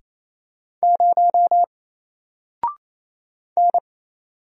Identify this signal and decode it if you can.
Morse code, heard as a keyed tone, about 27 wpm, 700 Hz. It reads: 0 N
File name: Single-Letter-Number-28wpm